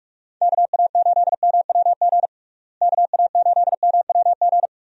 Morse code, heard as a keyed tone, about 46 wpm, 700 Hz. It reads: KA8MWG KA8MWG
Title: USA Call Signs - 6 Characters - Repeated 2x - 45wpm